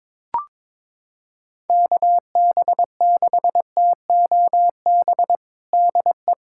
Morse code, heard as a keyed tone, about 22 wpm, 700 Hz. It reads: KB6TOB DE
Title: USA Callsign DE Callsign 22wpm